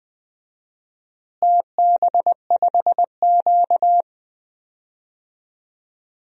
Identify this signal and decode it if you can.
Morse code, heard as a keyed tone, about 20 wpm, 700 Hz. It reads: TB5Q
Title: DXCC-100 Call Signs - 4 Characters - Repeated 3x - 20wpm